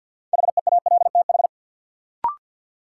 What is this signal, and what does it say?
Morse code, heard as a keyed tone, about 53 wpm, 700 Hz. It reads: HEALTH
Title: Sentences from Top 600 Words - New Word Drill - 50wpm